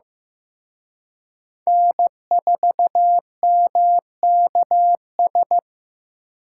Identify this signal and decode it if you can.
Morse code, heard as a keyed tone, about 18 wpm, 700 Hz. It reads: N4MKS
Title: Speed Racing - USA 1x3 Call Signs 15wpm